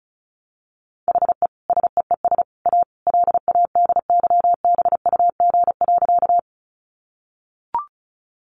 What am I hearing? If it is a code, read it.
Morse code, heard as a keyed tone, about 34 wpm, 700 Hz. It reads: HE SEES A LADYBUG.
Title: Additional Easy Phrases 35wpm